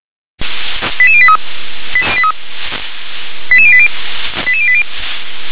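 Five-tone selective call (ZVEI), heard as quick runs of five tones, tones of about 70 ms. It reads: 88083 88080 88080
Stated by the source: ZVEI five-tone call 88759 received on SDR.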